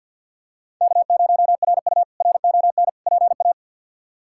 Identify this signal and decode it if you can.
Morse code, heard as a keyed tone, about 46 wpm, 700 Hz. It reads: K0RU RON PA
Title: K1USN SST QSO Practice - 50wpm